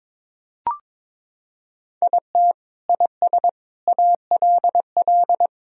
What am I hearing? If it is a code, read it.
Morse code, heard as a keyed tone, about 22 wpm, 700 Hz. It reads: IT IS ALL
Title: Sentences from Top 1000 Words - Review Part 4 - 22wpm